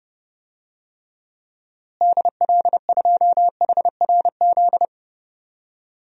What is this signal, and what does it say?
Morse code, heard as a keyed tone, about 30 wpm, 700 Hz. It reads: DL2HRZ
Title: DXCC-100 Call Signs - 6 Characters - Repeated 3x - 30wpm